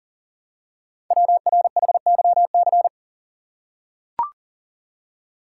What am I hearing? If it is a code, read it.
Morse code, heard as a keyed tone, about 39 wpm, 700 Hz. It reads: WRHYC
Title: Code Groups - 5 Letters - 40wpm